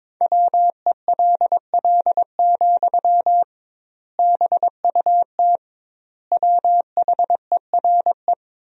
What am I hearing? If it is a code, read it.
Morse code, heard as a keyed tone, about 21 wpm, 700 Hz. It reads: WELL, BUT WHERE
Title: 2x Word-Spacing - Sentences from Top 500 Words - 22wpm